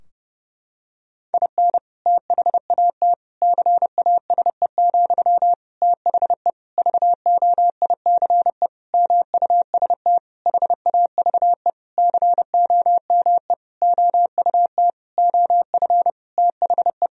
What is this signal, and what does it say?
Morse code, heard as a keyed tone, about 30 wpm, 700 Hz. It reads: IN THAT CASE, THE VOICE MUST HAVE COME OUT OF THE
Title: Sentences from Top 1000 Words - Review Part 2 - 30wpm